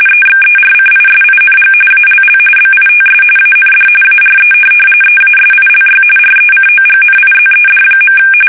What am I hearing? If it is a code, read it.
RTTY, about 45.45 baud. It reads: EXERCISING ALL BITS AND ARE EASILY RECOGNIZED.
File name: fsk850 / teletype-RTTY